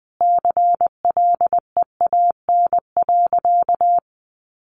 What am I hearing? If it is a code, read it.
Morse code, heard as a keyed tone, about 20 wpm, 700 Hz. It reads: CLEAN.
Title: Sentences from Top 1100 Words 20wpm